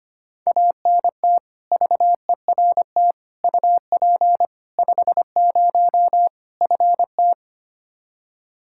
Morse code, heard as a keyed tone, about 25 wpm, 700 Hz. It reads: ANT VERT UP 50 FT
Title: Practice QSO - Exchange 2 - Info - 25wpm